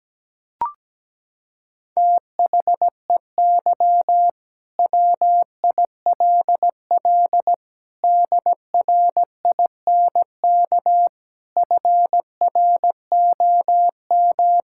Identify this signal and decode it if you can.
Morse code, heard as a keyed tone, about 17 wpm, 700 Hz. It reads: THEY WILL DRINK FROM
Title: Sentences from Top 1600 Words 17wpm